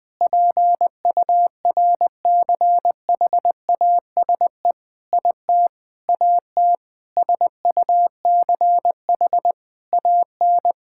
Morse code, heard as a keyed tone, about 21 wpm, 700 Hz. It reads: PURCHASE IT AT SUCH AN
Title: Sentences from Top 1000 Words - Review Part 5 - 20wpm